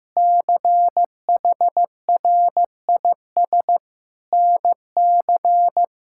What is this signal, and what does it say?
Morse code, heard as a keyed tone, about 15 wpm, 700 Hz. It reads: CHRIS NC
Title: K1USN SST QSO Practice - 15wpm